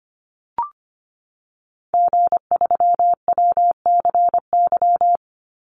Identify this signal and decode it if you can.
Morse code, heard as a keyed tone, about 25 wpm, 700 Hz. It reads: G3WCY
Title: Call Signs - 25wpm